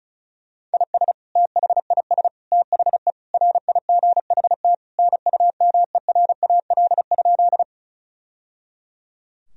Morse code, heard as a keyed tone, about 36 wpm, 700 Hz. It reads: IS THIS THE RIGHT NUMERAL?
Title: Sentences from Top 1000 Words - Review Part 2 - 35wpm